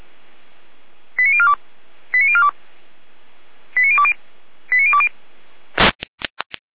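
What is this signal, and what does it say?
Five-tone selective call (ZVEI), heard as quick runs of five tones, tones of about 70 ms. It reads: 89042 89042 89020 89020
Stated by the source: ZVEI five-tone selective call received by SDR, from the feuerwehr-orlamuende.org fire brigade server.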